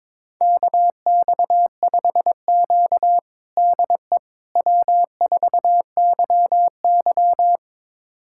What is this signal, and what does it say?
Morse code, heard as a keyed tone, about 22 wpm, 700 Hz. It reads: KX5Q DE W4YY
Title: USA Callsign DE Callsign - 4 Characters - 22wpm